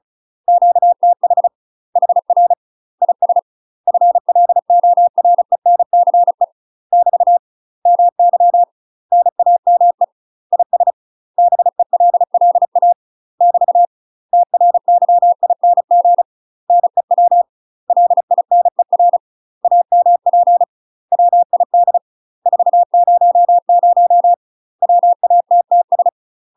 Morse code, heard as a keyed tone, about 34 wpm, 700 Hz. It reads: QTH HR IS FLORENCE <BT> MY NAME IS BELLA <BT> TRYING NEW LINER AMP WID 400 WATTS
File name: UK MASTER QSOs TRAINING / 35WPM / UK MASTER QSOs 0004